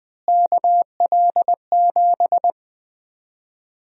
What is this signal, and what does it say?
Morse code, heard as a keyed tone, about 21 wpm, 700 Hz. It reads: KL7
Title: USA Call Sign Prefixes - Repeated 2x - 20wpm